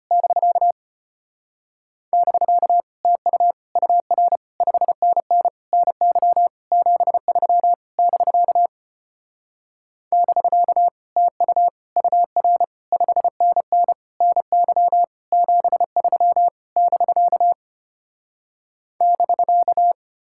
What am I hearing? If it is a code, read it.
Morse code, heard as a keyed tone, about 33 wpm, 700 Hz. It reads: <BK> <BK> TU UR 5NN NY 73 <BK> <BK> TU UR 5NN NY 73 <BK> <BK>